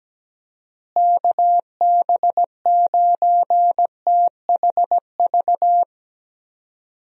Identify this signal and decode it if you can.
Morse code, heard as a keyed tone, about 17 wpm, 700 Hz. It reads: KB9THV
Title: USA Call Signs - 6 Characters - Repeated 2x - 17wpm